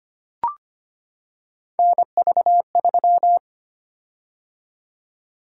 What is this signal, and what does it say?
Morse code, heard as a keyed tone, about 25 wpm, 700 Hz. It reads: NV3
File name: Three Letter-Number Combinations - 25wpm